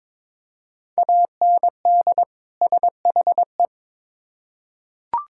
Morse code, heard as a keyed tone, about 22 wpm, 700 Hz. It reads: AND SHE
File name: Build A Phrase - 2-Word Phrases - 22wpm